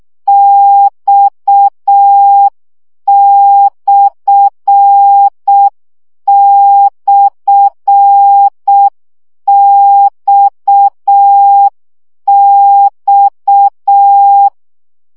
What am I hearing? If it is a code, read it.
Morse code, heard as a keyed tone, about 6 wpm, 800 Hz. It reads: X//XX
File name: CW gyakorlas x per 5 baud